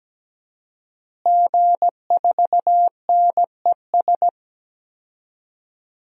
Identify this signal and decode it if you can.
Morse code, heard as a keyed tone, about 17 wpm, 700 Hz. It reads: G4NES
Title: DXCC-100 Call Signs - 5 Characters - Repeated 2x - 17wpm